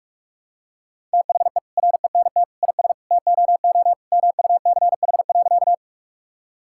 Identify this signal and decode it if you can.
Morse code, heard as a keyed tone, about 41 wpm, 700 Hz. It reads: THE RENT IS TOO MUCH.
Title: Sentences from Top 1000 Words - Review Part 5 - 45wpm